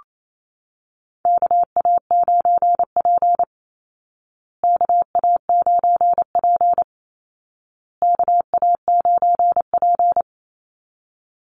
Morse code, heard as a keyed tone, about 28 wpm, 700 Hz. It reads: KA9P KA9P KA9P
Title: Call Signs Repeated 2x - 28wpm